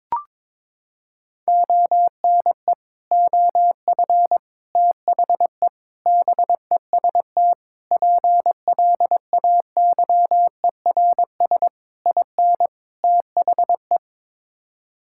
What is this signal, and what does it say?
Morse code, heard as a keyed tone, about 22 wpm, 700 Hz. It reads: ONE OF THE BEST PLAYERS IN THE
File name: practice / 22wpm / Sets of 7 Words - 22wpm